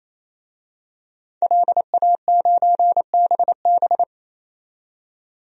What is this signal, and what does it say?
Morse code, heard as a keyed tone, about 28 wpm, 700 Hz. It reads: LA9BB
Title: DXCC-100 Call Signs - 5 Characters - 28wpm